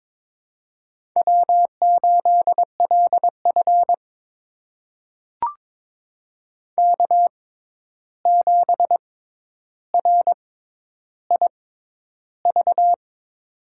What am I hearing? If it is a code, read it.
Morse code, heard as a keyed tone, about 22 wpm, spaced as for 11 wpm, 700 Hz. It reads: W8LF K7RIV
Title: USA Call Signs - 6x Character Spacing - 22wpm